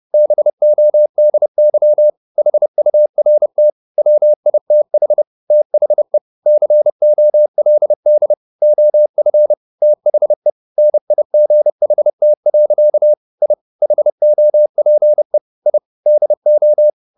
Morse code, heard as a keyed tone, about 30 wpm, 600 Hz. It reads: BODY HURT WITH THE COLD OF THE NIGHT. I HOPE I DO